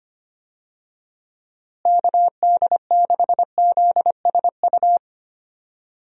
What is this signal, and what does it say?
Morse code, heard as a keyed tone, about 25 wpm, 700 Hz. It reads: KD6ZSU
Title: USA Call Signs - 6 Characters - Repeated 2x - 25wpm